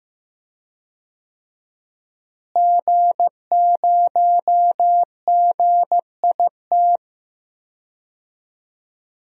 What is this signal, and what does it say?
Morse code, heard as a keyed tone, about 15 wpm, 700 Hz. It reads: G0GIT
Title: DXCC-100 Call Signs - 5 Characters - Repeated 2x - 15wpm